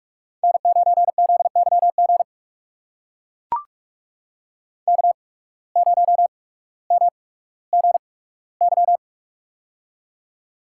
Morse code, heard as a keyed tone, about 44 wpm, spaced as for 19 wpm, 700 Hz. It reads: N9ZYG K0MGY
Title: USA Call Signs - 5 Characters - 8x Character Spacing - 45wpm